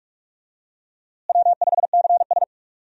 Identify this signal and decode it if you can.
Morse code, heard as a keyed tone, about 46 wpm, 700 Hz. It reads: W5CS
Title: USA Call Signs - 4 Characters - Repeated 2x - 45wpm